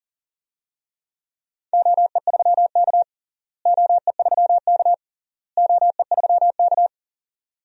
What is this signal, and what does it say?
Morse code, heard as a keyed tone, about 40 wpm, 700 Hz. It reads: OE3K OE3K OE3K
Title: DXCC-100 Call Signs - 4 Characters - Repeated 3x - 40wpm